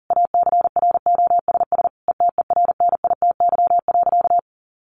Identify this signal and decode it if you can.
Morse code, heard as a keyed tone, about 40 wpm, 700 Hz. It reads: ACROSS ETERNITY.
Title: A Sound Of Thunder 40wpm